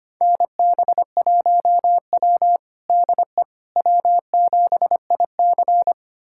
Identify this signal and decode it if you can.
Morse code, heard as a keyed tone, about 25 wpm, 700 Hz. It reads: NB1W DE W7IC
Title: USA Callsign DE Callsign - 4 Characters - 25wpm